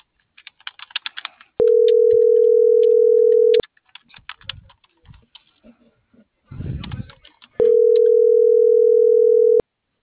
Ringback tone (North America), 440 + 480 Hz.